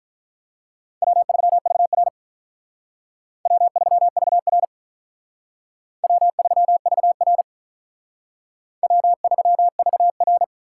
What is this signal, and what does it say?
Morse code, heard as a keyed tone, about 45 wpm, 700 Hz. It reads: W3VR W3VR W3VR W3VR